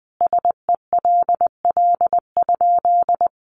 Morse code, heard as a keyed tone, about 20 wpm, 700 Hz. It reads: SELL?
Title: Sentences from Top 1100 Words 20wpm